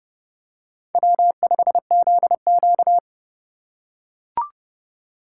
Morse code, heard as a keyed tone, about 30 wpm, 700 Hz. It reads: W5ZQ
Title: USA Call Signs - 4 Characters - 2x Character Spacing - 30wpm